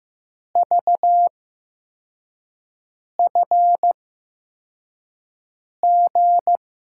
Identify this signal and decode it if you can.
Morse code, heard as a keyed tone, about 15 wpm, spaced as for 4 wpm, 700 Hz. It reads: VFG
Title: USA Call Signs - 6 Characters - 8x Character Spacing - 15wpm